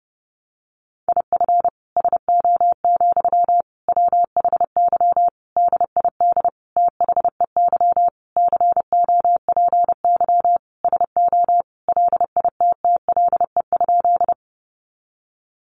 Morse code, heard as a keyed tone, about 30 wpm, 700 Hz. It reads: IF SO, WHY DID THEY COPY SO LITTLE?
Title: Sentences from Top 1400 Words 30wpm